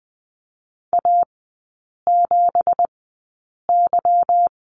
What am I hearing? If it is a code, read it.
Morse code, heard as a keyed tone, about 20 wpm, spaced as for 8 wpm, 700 Hz. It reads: A7Y
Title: 166 Lesson 27e - Number 7 - Three Characters 20wpm